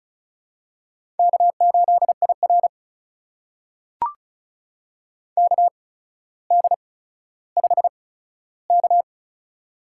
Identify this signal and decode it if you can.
Morse code, heard as a keyed tone, about 36 wpm, spaced as for 15 wpm, 700 Hz. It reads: K8IR KD5K
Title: USA Call Signs - 4 Characters - 8x Character Spacing - 35wpm